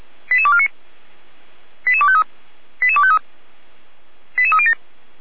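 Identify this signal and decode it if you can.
Five-tone selective call (ZVEI), heard as quick runs of five tones, tones of about 70 ms. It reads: 88249 88253 88253 88307
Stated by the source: ZVEI five-tone call 88209 received by SDR.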